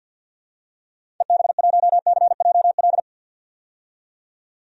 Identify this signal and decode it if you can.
Morse code, heard as a keyed tone, about 46 wpm, 700 Hz. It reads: EB1CJL
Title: DXCC-100 Call Signs - 6 Characters - 50wpm